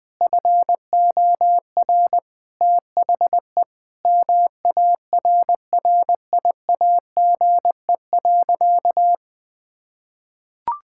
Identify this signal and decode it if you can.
Morse code, heard as a keyed tone, about 20 wpm, 700 Hz. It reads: FOR THE MARRIAGE.
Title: Sentences from Top 1000 Words - Review Part 6 - 20wpm